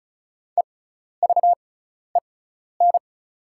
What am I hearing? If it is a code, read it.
Morse code, heard as a keyed tone, about 36 wpm, spaced as for 12 wpm, 700 Hz. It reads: EVEN
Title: 6x Character Spacing - Top 200 Words - 35wpm